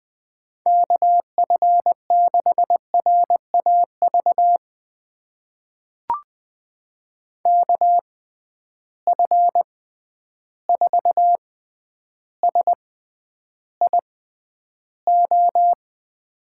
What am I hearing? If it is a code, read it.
Morse code, heard as a keyed tone, about 21 wpm, spaced as for 10 wpm, 700 Hz. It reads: KF6RAV KF4SIO
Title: USA Call Signs - 6 Characters - 6x Character Spacing - 20wpm